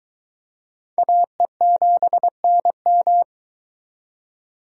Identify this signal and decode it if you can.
Morse code, heard as a keyed tone, about 25 wpm, 700 Hz. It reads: AE7NM